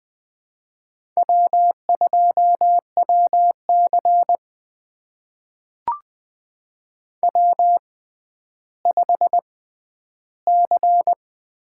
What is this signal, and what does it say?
Morse code, heard as a keyed tone, about 20 wpm, spaced as for 11 wpm, 700 Hz. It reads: W2WC W5C
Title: USA Call Signs - 4 Characters - 6x Character Spacing - 20wpm